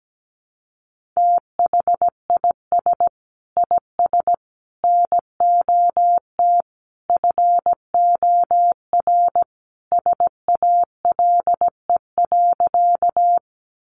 Morse code, heard as a keyed tone, about 17 wpm, 700 Hz. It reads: THIS IS NOT FOR SALE.
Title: Sentences from Top 1400 Words 17wpm